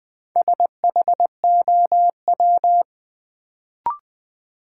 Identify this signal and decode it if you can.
Morse code, heard as a keyed tone, about 20 wpm, 700 Hz. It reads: SHOW
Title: Sentences from Top 200 Words - New Word Drill - 20wpm